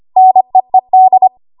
Morse code, heard as a keyed tone, about 25 wpm, 750 Hz. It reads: NEED
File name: Morse / 25WPM